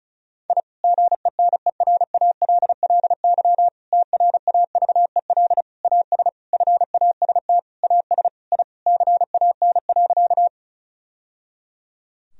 Morse code, heard as a keyed tone, about 34 wpm, 700 Hz. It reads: I GENERALLY TRAVEL AS FAST AS I CAN.
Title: Sentences from Top 1000 Words - Review Part 4 - 35wpm